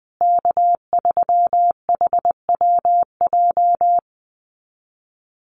Morse code, heard as a keyed tone, about 20 wpm, 700 Hz. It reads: K3HWJ
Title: USA Call Signs Repeated 3x 20wpm